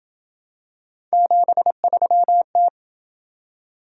Morse code, heard as a keyed tone, about 27 wpm, 700 Hz. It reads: 73T